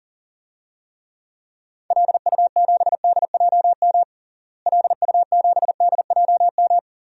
Morse code, heard as a keyed tone, about 40 wpm, 700 Hz. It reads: LU7DJM LU7DJM
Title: DXCC-100 Call Signs - 6 Characters - Repeated 2x - 40wpm